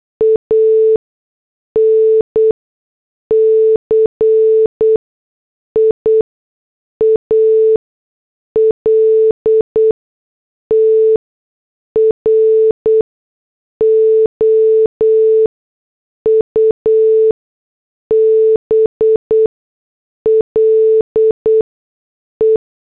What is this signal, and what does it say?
Morse code, heard as a keyed tone, about 8 wpm, spaced as for 6 wpm, 440 Hz. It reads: ANCIAL TROUBLE
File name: morse / headlines